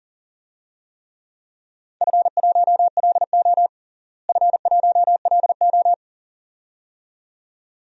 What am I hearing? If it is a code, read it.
Morse code, heard as a keyed tone, about 40 wpm, 700 Hz. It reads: F1LO F1LO
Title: DXCC-100 Call Signs - 4 Characters - Repeated 2x - 40wpm